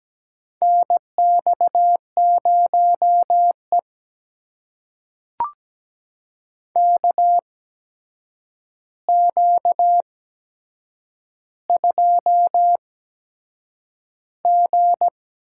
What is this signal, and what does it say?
Morse code, heard as a keyed tone, about 17 wpm, spaced as for 7 wpm, 700 Hz. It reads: NX0E KQ2G
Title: USA Call Signs - 4 Characters - 8x Character Spacing - 17wpm